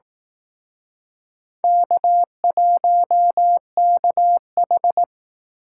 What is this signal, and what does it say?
Morse code, heard as a keyed tone, about 18 wpm, 700 Hz. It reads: K1KH
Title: Speed Racing - USA 1x2 Call Signs 15wpm